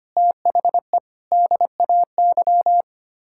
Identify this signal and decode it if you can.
Morse code, heard as a keyed tone, about 25 wpm, 700 Hz. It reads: THE DAY
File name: practice / 25wpm / Sentences with 3-Word Phrases - 25wpm